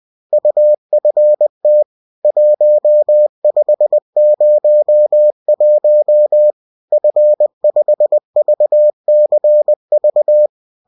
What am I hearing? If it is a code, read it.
Morse code, heard as a keyed tone, about 20 wpm, 600 Hz. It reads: UFT 1501 F5VCV